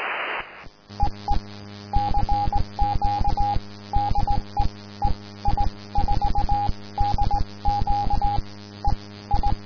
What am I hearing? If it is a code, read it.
Morse code, heard as a keyed tone, about 18 wpm, spaced as for 16 wpm, 820 Hz, with radio noise. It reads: EE CQ DE EI4DQ EI
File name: ei4dq / aurora 15 march 2012 / f5dqk